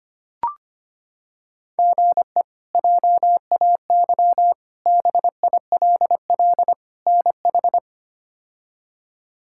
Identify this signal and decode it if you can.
Morse code, heard as a keyed tone, about 25 wpm, 700 Hz. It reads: GE JAY BILL NH
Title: K1USN SST QSO Practice - 25wpm